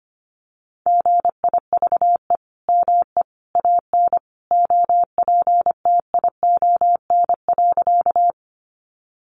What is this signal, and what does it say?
Morse code, heard as a keyed tone, about 25 wpm, 700 Hz. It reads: GIVE ME AN OPTION.
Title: Sentences from Top 1300 Words 25wpm